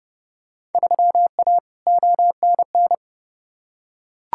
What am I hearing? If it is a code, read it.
Morse code, heard as a keyed tone, about 30 wpm, 700 Hz. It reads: 3A ONN
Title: Field Day Exchanges Encore - 30wpm